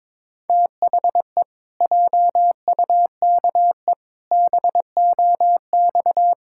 Morse code, heard as a keyed tone, about 22 wpm, 700 Hz. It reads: THE JUKE BOX
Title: Pangram Challenge v2 - 22wpm